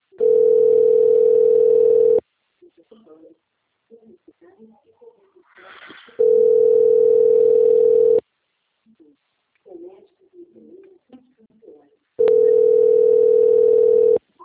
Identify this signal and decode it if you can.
Ringback tone (North America), 440 + 480 Hz, with speech.
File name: rg-1600-unknown- - - .23454